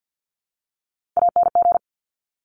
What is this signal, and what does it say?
Morse code, heard as a keyed tone, about 50 wpm, 700 Hz. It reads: ANG
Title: Top Three Letter Combinations in Words 50wpm